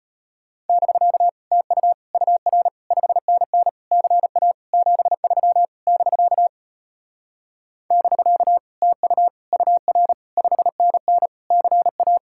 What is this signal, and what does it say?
Morse code, heard as a keyed tone, about 33 wpm, 700 Hz. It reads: <BK> TU UR 5NN CA 73 <BK> <BK> TU UR 5NN CA